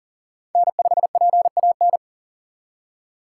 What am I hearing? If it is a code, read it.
Morse code, heard as a keyed tone, about 51 wpm, 700 Hz. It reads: N5PAN